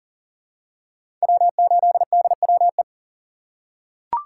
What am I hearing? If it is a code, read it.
Morse code, heard as a keyed tone, about 40 wpm, 700 Hz. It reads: W8DWE
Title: USA Call Signs - 4x Character Spacing - 40wpm